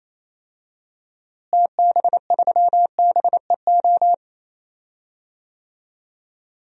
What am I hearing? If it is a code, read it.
Morse code, heard as a keyed tone, about 27 wpm, 700 Hz. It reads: TB3BEO
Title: DXCC-100 Call Signs - 6 Characters - Repeated 2x - 28wpm